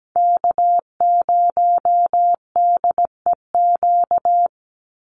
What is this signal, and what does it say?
Morse code, heard as a keyed tone, about 17 wpm, 700 Hz. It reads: K0DEQ
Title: Call Signs Repeated 2x - 17wpm